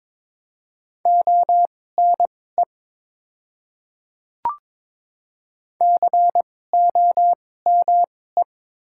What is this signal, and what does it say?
Morse code, heard as a keyed tone, about 22 wpm, spaced as for 16 wpm, 700 Hz. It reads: ONE COME
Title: 2x Character Spacing - Top 100 Words - 22wpm